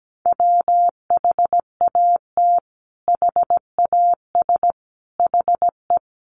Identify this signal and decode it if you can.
Morse code, heard as a keyed tone, about 17 wpm, 700 Hz. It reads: WHAT HAS HE
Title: Sentences from Top 500 Words 17wpm